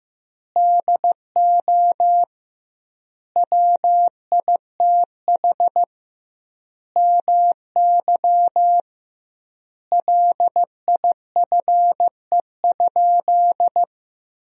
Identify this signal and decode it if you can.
Morse code, heard as a keyed tone, about 16 wpm, 700 Hz. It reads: DO WITH MY LIFE?
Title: 2x Word-Spacing - Sentences from Top 300 Words - 15wpm